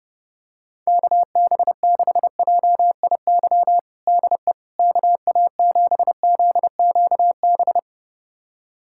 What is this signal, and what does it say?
Morse code, heard as a keyed tone, about 30 wpm, 700 Hz. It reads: KB6JIY DE KA7ZQB
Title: USA Callsign DE Callsign - 6 Characters - 30wpm